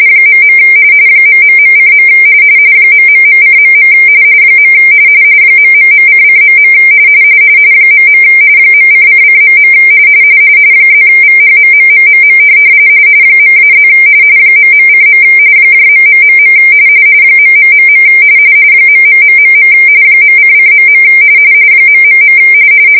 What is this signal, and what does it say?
RTTY, about 45.45 baud. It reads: .. ...:. ...:.. ...::.. .. .:::.. +VHMMMMM .-+ ..:. .... .... ..... .::.. ...